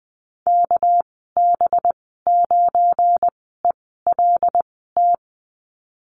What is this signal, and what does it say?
Morse code, heard as a keyed tone, about 20 wpm, spaced as for 14 wpm, 700 Hz. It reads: KB9ELT
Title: 245 Lesson 39g - Letter z - Call Signs 20wpm